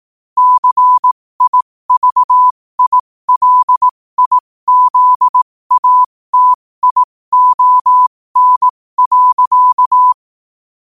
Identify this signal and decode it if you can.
Morse code, heard as a keyed tone, about 18 wpm, spaced as for 15 wpm, 1000 Hz. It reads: CIVILIZATION.
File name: qotdfiles / 15wpm